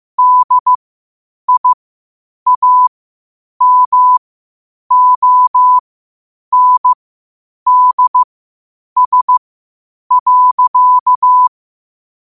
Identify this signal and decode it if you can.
Morse code, heard as a keyed tone, about 15 wpm, spaced as for 8 wpm, 1000 Hz. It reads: DIAMONDS.
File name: qotdfiles / 8wpm